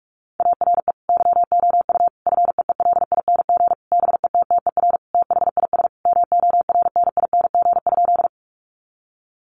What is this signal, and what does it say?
Morse code, heard as a keyed tone, about 44 wpm, 700 Hz. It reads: ARE YOU FEELING BETTER THIS MORNING?
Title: Sentences from Top 1200 Words 45wpm